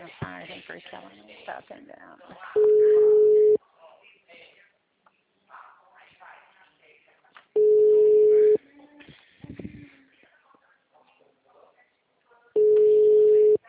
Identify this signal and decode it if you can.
Ringback tone (Europe, 425 Hz), with speech.
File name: out- -2008- - - .57251